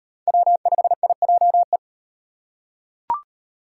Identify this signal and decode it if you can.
Morse code, heard as a keyed tone, about 38 wpm, 700 Hz. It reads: W5IJE